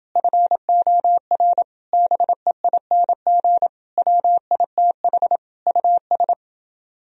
Morse code, heard as a keyed tone, about 30 wpm, 700 Hz. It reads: FOR BEING WITH US